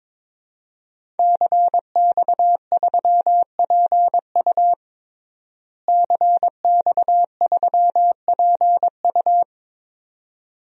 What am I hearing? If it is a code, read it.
Morse code, heard as a keyed tone, about 22 wpm, 700 Hz. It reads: CX3PU CX3PU
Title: DXCC-100 Call Signs - 5 Characters - Repeated 2x - 22wpm